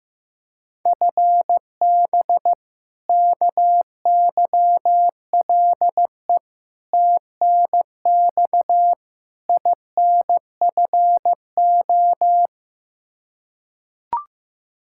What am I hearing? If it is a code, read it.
Morse code, heard as a keyed tone, about 15 wpm, 700 Hz. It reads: FB KYLE TNX INFO
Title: Practice QSO - Exchange 3 - More Info - 15wpm